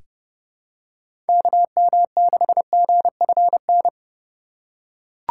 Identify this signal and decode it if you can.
Morse code, heard as a keyed tone, about 30 wpm, 700 Hz. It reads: KM6GFN
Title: USA Call Signs - 6 Characters - 3x Character Spacing - 30wpm